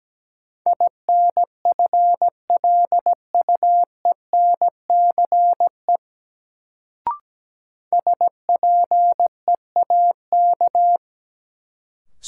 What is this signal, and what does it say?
Morse code, heard as a keyed tone, about 17 wpm, 700 Hz. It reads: INFLUENCE SPEAK
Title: Sentences from Top 1100 Words - New Word Drill Alt - 17wpm